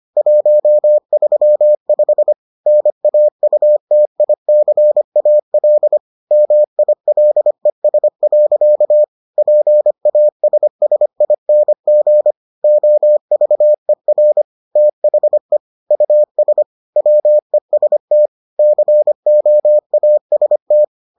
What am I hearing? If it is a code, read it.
Morse code, heard as a keyed tone, about 25 wpm, 600 Hz. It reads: 135 NAUTICAL MILES. PASSING OVER THE US WEST COAST